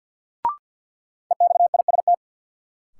Morse code, heard as a keyed tone, about 46 wpm, 700 Hz. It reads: EXIST
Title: Sentences from Top 1000 Words - Review Part 5 - New Word Drill - 50wpm